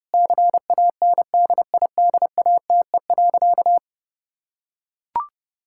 Morse code, heard as a keyed tone, about 30 wpm, 700 Hz. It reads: CANDIDATE.
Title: Sentences from Top 1600 Words 30wpm